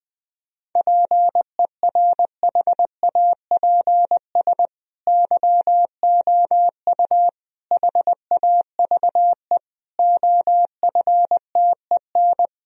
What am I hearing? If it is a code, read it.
Morse code, heard as a keyed tone, about 21 wpm, 700 Hz. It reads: PERHAPS YOU HAVE OFTEN
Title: Sentences from Top 1000 Words - Review Part 1 - 20wpm